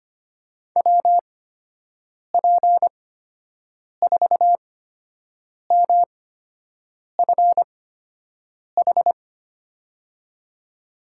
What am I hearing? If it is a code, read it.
Morse code, heard as a keyed tone, about 25 wpm, spaced as for 7 wpm, 700 Hz. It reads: WP4MFH
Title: USA Call Signs - 6 Characters - 8x Character Spacing - 25wpm